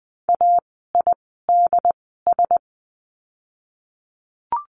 Morse code, heard as a keyed tone, about 20 wpm, spaced as for 15 wpm, 700 Hz. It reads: AIDS
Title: 068 Lesson 12f - Letter d - Words 20wpm